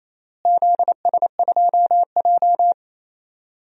Morse code, heard as a keyed tone, about 28 wpm, 700 Hz. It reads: ZS2J
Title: DXCC-100 Call Signs - 4 Characters - Repeated 3x - 28wpm